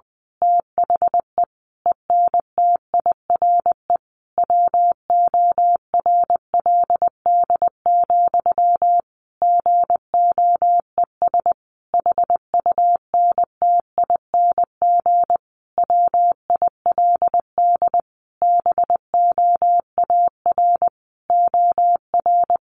Morse code, heard as a keyed tone, about 20 wpm, 700 Hz. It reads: THE ENTIRE WORLD, GOES HUNTING WILD BOAR OR
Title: A Sound Of Thunder 20wpm